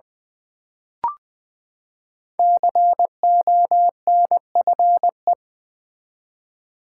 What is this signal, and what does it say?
Morse code, heard as a keyed tone, about 20 wpm, 700 Hz. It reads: CONFE
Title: Build A Word - 5 Letters At A Time - 10-Letter Words - 20wpm